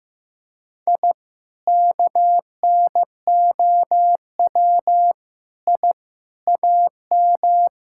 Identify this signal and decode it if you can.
Morse code, heard as a keyed tone, about 15 wpm, 700 Hz. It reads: I KNOW I AM
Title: Sentences from Top 1000 Words - Review Part 5 - 15wpm